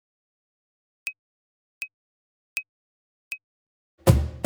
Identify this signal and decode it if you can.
There is no code.